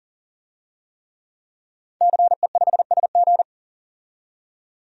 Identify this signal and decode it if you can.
Morse code, heard as a keyed tone, about 40 wpm, 700 Hz. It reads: CE5SG
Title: DXCC-100 Call Signs - 5 Characters - 40wpm